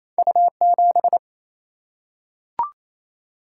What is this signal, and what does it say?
Morse code, heard as a keyed tone, about 28 wpm, 700 Hz. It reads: U7
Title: Single-Letter Single-Number 28wpm